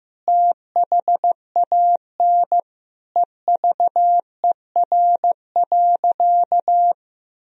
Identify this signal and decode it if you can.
Morse code, heard as a keyed tone, about 15 wpm, 700 Hz. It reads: THAN EVER.
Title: Sentences from Top 1000 Words - Review Part 2 - 15wpm